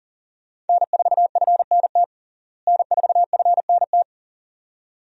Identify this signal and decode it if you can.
Morse code, heard as a keyed tone, about 40 wpm, 700 Hz. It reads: N4FNT N4FNT
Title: USA Call Signs - 5 Characters - Repeated 2x - 40wpm